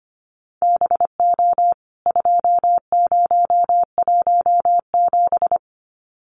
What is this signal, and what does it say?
Morse code, heard as a keyed tone, about 25 wpm, 700 Hz. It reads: BO 2017
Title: CWT Exchange 25wpm